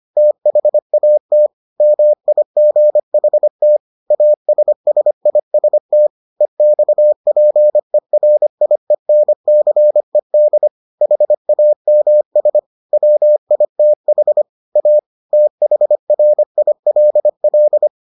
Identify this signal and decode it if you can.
Morse code, heard as a keyed tone, about 25 wpm, 600 Hz. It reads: THAT MIGHT ASSIST EXPERIENCED HAMS WITH A THRILL